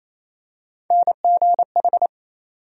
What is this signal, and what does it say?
Morse code, heard as a keyed tone, about 28 wpm, 700 Hz. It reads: NGH
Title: USA Call Sign Postfixes - 3 Letters 28wpm